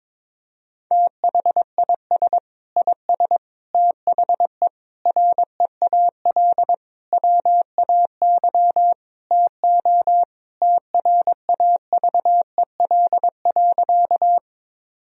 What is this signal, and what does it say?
Morse code, heard as a keyed tone, about 21 wpm, 700 Hz. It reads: THIS IS THE REAL WAY TO TRAVEL.
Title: Sentences from Top 1000 Words - Review Part 2 - 22wpm